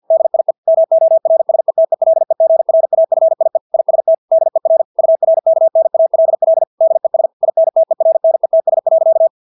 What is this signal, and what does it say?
Morse code, heard as a keyed tone, about 50 wpm, 650 Hz. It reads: DIE MORSETELEGRAFIE IST DER URKNALL DES INTERNETS.
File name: dk5ke / Motto 250BpM